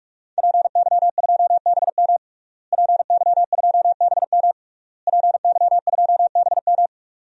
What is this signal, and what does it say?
Morse code, heard as a keyed tone, about 41 wpm, 700 Hz. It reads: PY2BM PY2BM PY2BM
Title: DXCC-100 Call Signs - 5 Characters - Repeated 3x - 45wpm